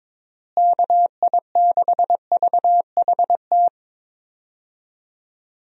Morse code, heard as a keyed tone, about 22 wpm, 700 Hz. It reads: KI6VHT
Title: USA Call Signs - 6 Characters - Repeated 3x - 22wpm